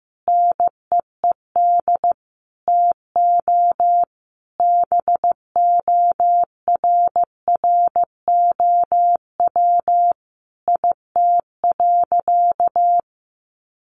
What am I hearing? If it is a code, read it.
Morse code, heard as a keyed tone, about 15 wpm, 700 Hz. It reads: NEED TO BORROW IT.
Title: Sentences from Top 1500 Words 15wpm